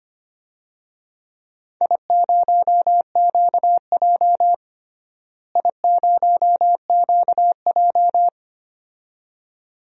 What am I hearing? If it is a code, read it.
Morse code, heard as a keyed tone, about 25 wpm, 700 Hz. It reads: I0QJ I0QJ
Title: DXCC-100 Call Signs - 4 Characters - Repeated 2x - 25wpm